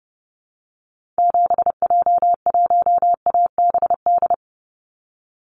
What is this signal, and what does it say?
Morse code, heard as a keyed tone, about 30 wpm, 700 Hz. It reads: 7J1ABD
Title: Call Signs Repeated 3x - 30wpm